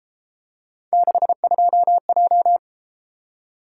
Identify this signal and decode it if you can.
Morse code, heard as a keyed tone, about 32 wpm, 700 Hz. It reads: 62J